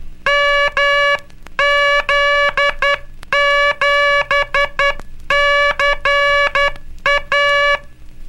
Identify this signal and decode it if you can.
Morse code, heard as a keyed tone, about 12 wpm, 1700 Hz, with noise. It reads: MZ7CA
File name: cursos / Curso CW mp3 / CW Aula - 17